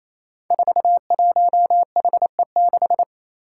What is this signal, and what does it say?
Morse code, heard as a keyed tone, about 28 wpm, 700 Hz. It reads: 41HE6
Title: Five Challenging Letter-Number Combinations 28wpm